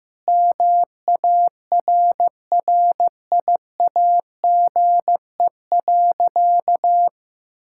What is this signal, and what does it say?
Morse code, heard as a keyed tone, about 15 wpm, 700 Hz. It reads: MARRIAGE.
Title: Sentences from Top 1000 Words - Review Part 5 - 15wpm